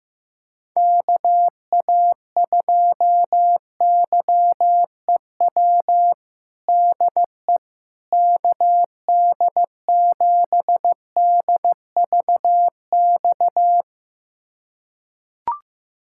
Morse code, heard as a keyed tone, about 15 wpm, 700 Hz. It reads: KA2YEW DE KD7DVX
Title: USA Callsign DE Callsign - 6 Characters - 15wpm